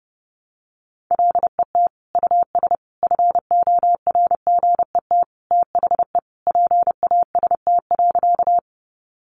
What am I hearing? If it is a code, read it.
Morse code, heard as a keyed tone, about 30 wpm, 700 Hz. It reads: LET US FORGET THE PAST.
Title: Sentences from Top 1200 Words 30wpm